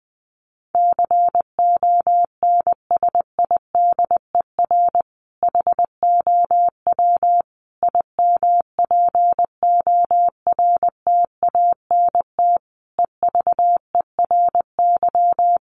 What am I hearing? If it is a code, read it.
Morse code, heard as a keyed tone, about 20 wpm, 700 Hz. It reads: CONSIDER HOW IMPORTANT EVERY
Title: Sentences from Top 1000 Words 20wpm